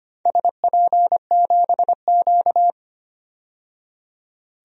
Morse code, heard as a keyed tone, about 25 wpm, 690 Hz. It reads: SP7Q
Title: DXCC-100 Call Signs - 4 Characters - 25wpm